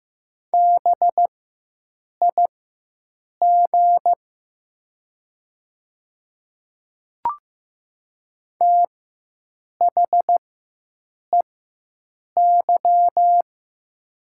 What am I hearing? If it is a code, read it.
Morse code, heard as a keyed tone, about 15 wpm, spaced as for 7 wpm, 690 Hz. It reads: BIG THEY
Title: 4x Character Spacing - Top 200 Words - 15wpm